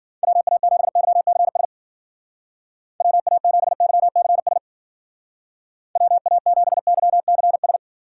Morse code, heard as a keyed tone, about 50 wpm, 700 Hz. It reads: WA7YCS WA7YCS WA7YCS
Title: Speed Racing - USA 2x3 Call Signs 40wpm